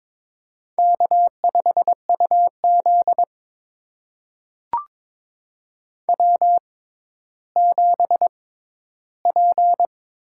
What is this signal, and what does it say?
Morse code, heard as a keyed tone, about 23 wpm, spaced as for 13 wpm, 700 Hz. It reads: K5UZ W7P
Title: USA Call Signs - 4 Characters - 6x Character Spacing - 22wpm